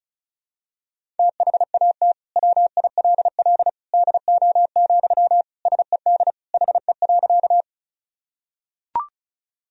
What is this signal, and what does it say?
Morse code, heard as a keyed tone, about 34 wpm, 700 Hz. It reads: THAT WILL DO, SED HE.
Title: Sentences from Top 100 Words and Abbreviations 35wpm